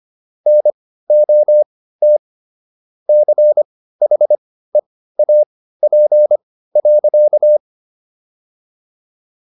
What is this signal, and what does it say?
Morse code, heard as a keyed tone, about 25 wpm, spaced as for 15 wpm, 600 Hz. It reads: NOT CHEAP.